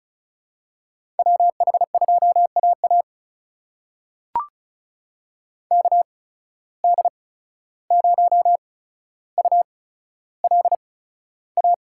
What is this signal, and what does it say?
Morse code, heard as a keyed tone, about 36 wpm, spaced as for 14 wpm, 700 Hz. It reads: WH2AA KD0ULA
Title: USA Call Signs - 8x Character Spacing - 35wpm